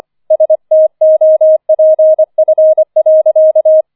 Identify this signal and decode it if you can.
Morse code, heard as a keyed tone, about 25 wpm, 630 Hz. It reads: STOPF.
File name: 004 Vierter Streich 25wpm